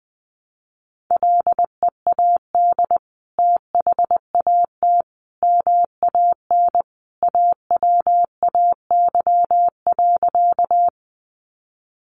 Morse code, heard as a keyed tone, about 20 wpm, 700 Hz. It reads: LEAD THAT MAN AWAY.
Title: Sentences from Top 600 Words 20wpm